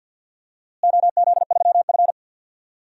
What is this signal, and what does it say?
Morse code, heard as a keyed tone, about 47 wpm, 700 Hz. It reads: OZ3F
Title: DXCC-100 Call Signs - 4 Characters - Repeated 3x - 50wpm